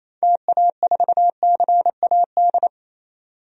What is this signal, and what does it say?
Morse code, heard as a keyed tone, about 27 wpm, 700 Hz. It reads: TA4CAD
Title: DXCC-100 Call Signs - 6 Characters - 28wpm